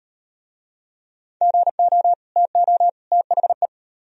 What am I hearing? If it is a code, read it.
Morse code, heard as a keyed tone, about 38 wpm, 700 Hz. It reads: GO TO THE